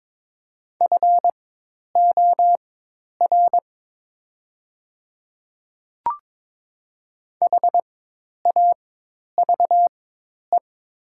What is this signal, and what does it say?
Morse code, heard as a keyed tone, about 22 wpm, spaced as for 10 wpm, 700 Hz. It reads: FOR HAVE
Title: 4x Character Spacing - Top 300 Words - 22wpm